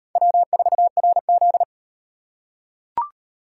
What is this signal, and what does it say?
Morse code, heard as a keyed tone, about 36 wpm, 700 Hz. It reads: W4RZ